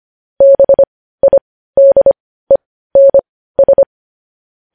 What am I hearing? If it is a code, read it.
Morse code, heard as a keyed tone, about 25 wpm, spaced as for 15 wpm, 550 Hz. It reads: BIDENS
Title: News Headlines at 15